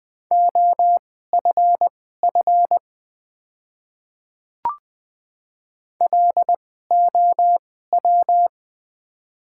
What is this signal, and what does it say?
Morse code, heard as a keyed tone, about 20 wpm, spaced as for 14 wpm, 700 Hz. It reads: OFF LOW
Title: 2x Character Spacing - Top 200 Words - 20wpm